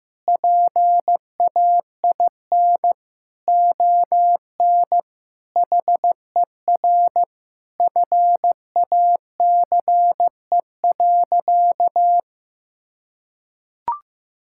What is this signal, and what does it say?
Morse code, heard as a keyed tone, about 15 wpm, 700 Hz. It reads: PAIN ON HER FACE.
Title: Sentences from Top 1600 Words 15wpm